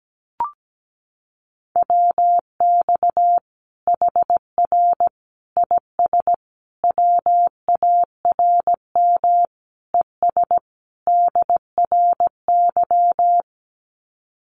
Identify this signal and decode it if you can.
Morse code, heard as a keyed tone, about 17 wpm, 700 Hz. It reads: WX HR IS WARM ES DRY
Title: Common QSO Phrases 17wpm